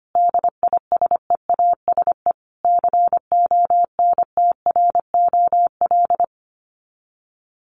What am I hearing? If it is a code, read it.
Morse code, heard as a keyed tone, about 25 wpm, 700 Hz. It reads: DISEASE CONTROL
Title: Sets of 5 Words Encore2 25wpm